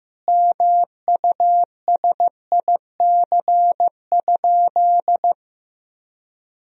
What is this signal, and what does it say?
Morse code, heard as a keyed tone, about 15 wpm, 700 Hz. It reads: MUSIC?
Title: Sentences from Top 300 Words and Abbreviations 15wpm